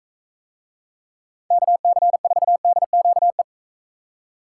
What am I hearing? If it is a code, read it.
Morse code, heard as a keyed tone, about 44 wpm, 700 Hz. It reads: KC4DQE